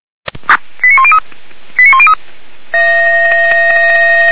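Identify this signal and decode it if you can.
Five-tone selective call (ZVEI), heard as quick runs of five tones, tones of about 70 ms. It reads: 89103 89103
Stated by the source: ZVEI five-tone selective call received on an SDR at Feuerwehr Orlamünde.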